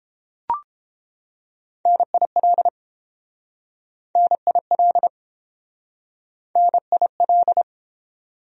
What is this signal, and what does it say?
Morse code, heard as a keyed tone, about 30 wpm, 700 Hz. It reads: NIL NIL NIL